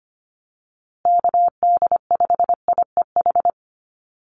Morse code, heard as a keyed tone, about 25 wpm, 700 Hz. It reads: KD5IEH
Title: USA Call Signs Repeated 2x 25wpm